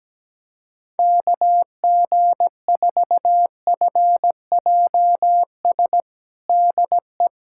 Morse code, heard as a keyed tone, about 17 wpm, 700 Hz. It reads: KG4FJS DE
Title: USA Callsign DE Callsign 17wpm